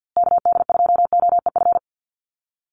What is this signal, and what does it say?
Morse code, heard as a keyed tone, about 50 wpm, 700 Hz. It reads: KD2OEF
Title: USA Call Signs Repeated 3x 50wpm